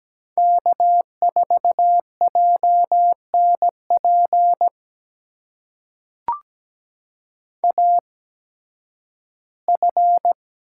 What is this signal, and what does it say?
Morse code, heard as a keyed tone, about 17 wpm, spaced as for 11 wpm, 700 Hz. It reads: K4JNP AF
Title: USA Call Signs - 5 Characters - 8x Character Spacing - 17wpm